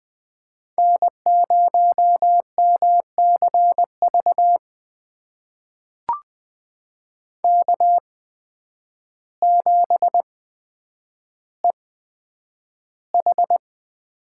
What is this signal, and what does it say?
Morse code, heard as a keyed tone, about 20 wpm, spaced as for 9 wpm, 700 Hz. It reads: N0MCV K7EH
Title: USA Call Signs - 5 Characters - 8x Character Spacing - 20wpm